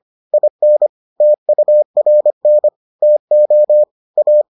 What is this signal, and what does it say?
Morse code, heard as a keyed tone, about 25 wpm, 600 Hz. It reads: IN TURN TO A